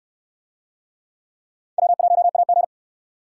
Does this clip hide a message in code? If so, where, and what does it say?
Yes: Morse code from 1.8 to 2.6 s, heard as a keyed tone, about 68 wpm, 700 Hz. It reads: W1AG